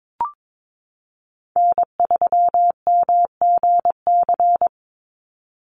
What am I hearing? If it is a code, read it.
Morse code, heard as a keyed tone, about 22 wpm, 700 Hz. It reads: N3MGC
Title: USA Call Signs 22wpm